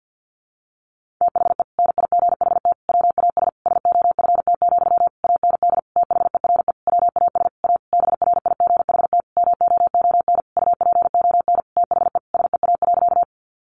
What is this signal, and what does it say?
Morse code, heard as a keyed tone, about 50 wpm, 700 Hz. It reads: THE NIGHT WAS SOFT, AND THERE WAS A BRIGHT MOON UPON THE SEA.
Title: Sentences from Top 1500 Words 50wpm